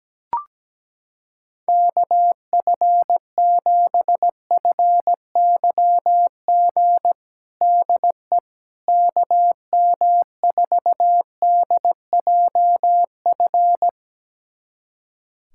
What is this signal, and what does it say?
Morse code, heard as a keyed tone, about 17 wpm, 700 Hz. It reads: KF7FYG DE KM4DJF
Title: USA Callsign DE Callsign 17wpm